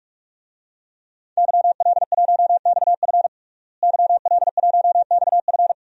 Morse code, heard as a keyed tone, about 46 wpm, 700 Hz. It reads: YL1XF YL1XF
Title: DXCC-100 Call Signs - 5 Characters - Repeated 3x - 45wpm